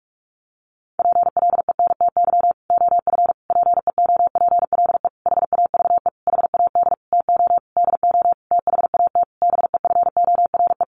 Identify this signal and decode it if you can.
Morse code, heard as a keyed tone, about 44 wpm, 700 Hz. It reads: PLENTY OF PEOPLE HAVE HAD TO DO THAT BEFORE
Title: Sentences from Top 1300 Words 45wpm